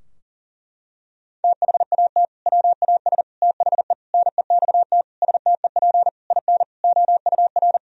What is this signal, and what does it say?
Morse code, heard as a keyed tone, about 40 wpm, 700 Hz. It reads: THAT WAS THE NEXT STEP IN OUR
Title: Sentences from Top 1000 Words - Review Part 4 - 40wpm